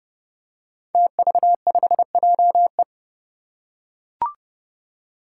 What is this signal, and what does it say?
Morse code, heard as a keyed tone, about 30 wpm, 700 Hz. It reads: TV5JE
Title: Five Challenging Letter-Number Combinations 30wpm